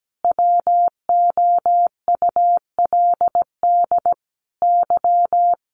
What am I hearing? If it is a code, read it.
Morse code, heard as a keyed tone, about 17 wpm, 700 Hz. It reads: WOULD Y
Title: Sentences from Top 500 Words 17wpm